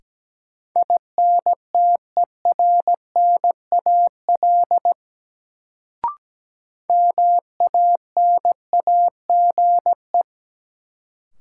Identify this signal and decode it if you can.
Morse code, heard as a keyed tone, about 17 wpm, 700 Hz. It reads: INTERNAL MANAGE
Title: Sentences from Top 1400 Words - New Word Drill - 17wpm